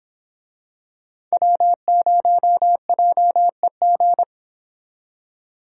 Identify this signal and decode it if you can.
Morse code, heard as a keyed tone, about 26 wpm, 700 Hz. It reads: W0JEG
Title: Speed Racing - USA 1x3 Call Signs 22wpm